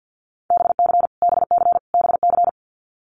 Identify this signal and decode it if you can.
Morse code, heard as a keyed tone, about 50 wpm, 700 Hz. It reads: BC BC BC
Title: Canadian Provinces and Territories - Repeated 3x - 50wpm